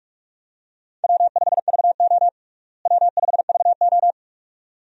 Morse code, heard as a keyed tone, about 43 wpm, 700 Hz. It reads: W5VO W5VO
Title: USA Call Signs - 4 Characters - Repeated 2x - 45wpm